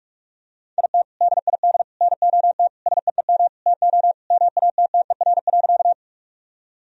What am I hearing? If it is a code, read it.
Morse code, heard as a keyed tone, about 46 wpm, 700 Hz. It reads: IT DID NOT SEEM TO MATTER.
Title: Sentences from Top 1000 Words - Review Part 1 - 45wpm